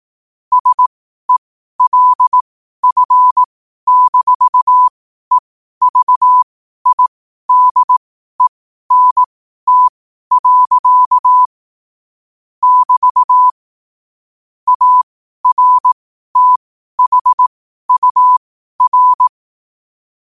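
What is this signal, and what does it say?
Morse code, heard as a keyed tone, about 18 wpm, spaced as for 12 wpm, 1000 Hz. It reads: SELF-EVIDENT. <BT> ARTHUR
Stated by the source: Morse code quote of the day at 12 wpm.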